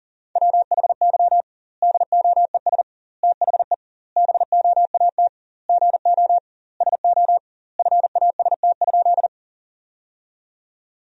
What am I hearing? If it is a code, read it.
Morse code, heard as a keyed tone, about 40 wpm, 700 Hz. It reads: WHY DOES THE BOAT GO SO FAST?
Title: 2x Word-Spacing - Sentences from Top 500 Words - 40wpm